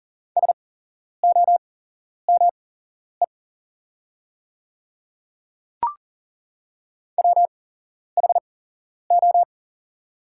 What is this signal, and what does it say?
Morse code, heard as a keyed tone, about 40 wpm, spaced as for 11 wpm, 700 Hz. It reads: SOME WHO
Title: 8x Character Spacing - Top 100 Words - 40wpm